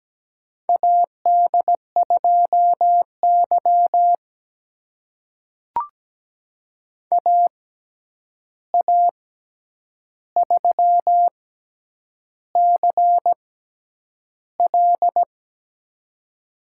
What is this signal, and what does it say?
Morse code, heard as a keyed tone, about 17 wpm, spaced as for 8 wpm, 700 Hz. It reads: AD2Y AA3CL
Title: USA Call Signs - 6x Character Spacing - 17wpm